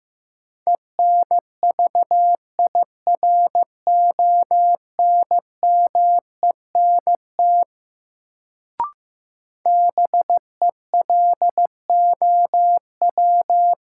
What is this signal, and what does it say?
Morse code, heard as a keyed tone, about 15 wpm, 700 Hz. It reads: ENVIRONMENT BELOW
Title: Sentences from Top 1000 Words - Review Part 5 - New Word Drill - 15wpm